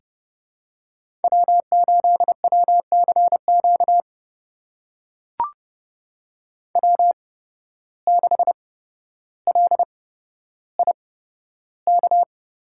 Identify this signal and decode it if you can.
Morse code, heard as a keyed tone, about 31 wpm, spaced as for 13 wpm, 700 Hz. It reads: W8WCQ W6LIK
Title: USA Call Signs - 5 Characters - 8x Character Spacing - 30wpm